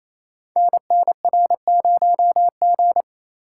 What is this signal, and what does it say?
Morse code, heard as a keyed tone, about 29 wpm, 700 Hz. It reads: NNR0G
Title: Code Groups - 5 Letter:Numbers - 28wpm